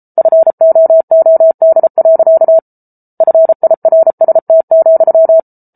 Morse code, heard as a keyed tone, about 34 wpm, 650 Hz. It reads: FOOD. FIRST,